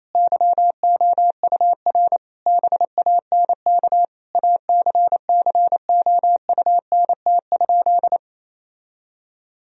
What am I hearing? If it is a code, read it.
Morse code, heard as a keyed tone, about 28 wpm, 700 Hz. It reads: YOUR BANK ACCOUNT?
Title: Sentences from Top 1000 Words - Review Part 4 - 28wpm